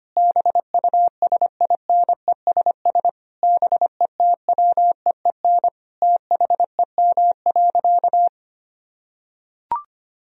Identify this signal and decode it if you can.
Morse code, heard as a keyed tone, about 25 wpm, 700 Hz. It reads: BUSINESS BETWEEN THEM.
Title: Sentences from Top 1000 Words - Review Part 1 - 25wpm